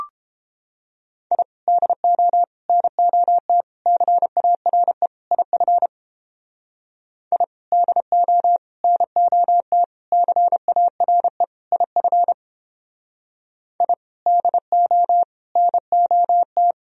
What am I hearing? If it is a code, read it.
Morse code, heard as a keyed tone, about 29 wpm, 700 Hz. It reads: I DO NOT CARE IF I DO NOT CARE IF I DO NOT